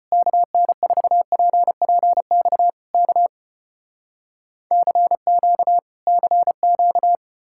Morse code, heard as a keyed tone, about 36 wpm, 700 Hz. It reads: KN4PPX K CQ CQ